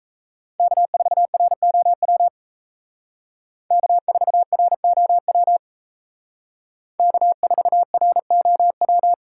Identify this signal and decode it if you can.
Morse code, heard as a keyed tone, about 34 wpm, 700 Hz. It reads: K4ROW K4ROW K4ROW
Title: Speed Racing - USA 1x3 Call Signs 28wpm